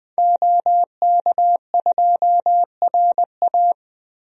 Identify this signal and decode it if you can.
Morse code, heard as a keyed tone, about 20 wpm, 700 Hz. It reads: OK2RA
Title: DXCC-100 Call Signs - 5 Characters - Repeated 2x - 20wpm